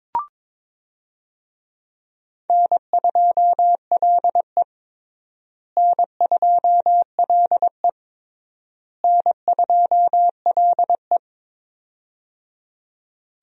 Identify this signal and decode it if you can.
Morse code, heard as a keyed tone, about 22 wpm, 700 Hz. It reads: N2LE N2LE N2LE
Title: USA Call Signs - 4 Characters - Repeated 3x - 22wpm